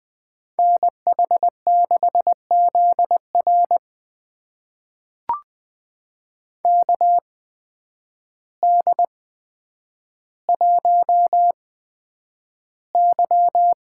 Morse code, heard as a keyed tone, about 20 wpm, spaced as for 9 wpm, 700 Hz. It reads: NH6ZR KD1Y
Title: USA Call Signs - 5 Characters - 8x Character Spacing - 20wpm